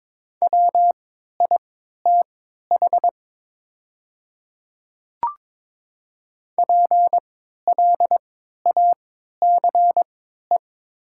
Morse code, heard as a keyed tone, about 22 wpm, spaced as for 12 wpm, 700 Hz. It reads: WITH PLACE
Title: 3x Character Spacing - Top 200 Words - 22wpm